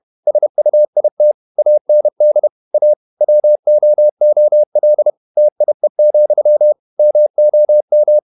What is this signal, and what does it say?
Morse code, heard as a keyed tone, about 31 wpm, 600 Hz. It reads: SUIT AND A WOOL TIE, MOM
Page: Never 31wpm | CW med Gnister